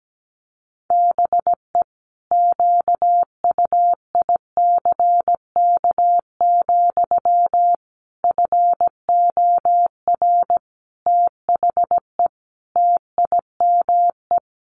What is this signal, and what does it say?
Morse code, heard as a keyed tone, about 17 wpm, 700 Hz. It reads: BE QUICK, FOR THE TIME
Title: Sentences from Top 500 Words 17wpm